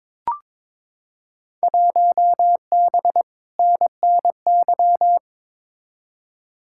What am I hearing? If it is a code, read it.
Morse code, heard as a keyed tone, about 23 wpm, 700 Hz. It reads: 1B NNY
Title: Field Day Exchanges - 22wpm